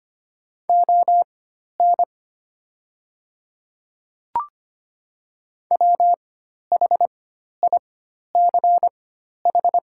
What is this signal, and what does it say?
Morse code, heard as a keyed tone, about 25 wpm, spaced as for 12 wpm, 700 Hz. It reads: ON WHICH
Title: 4x Character Spacing - Top 100 Words - 25wpm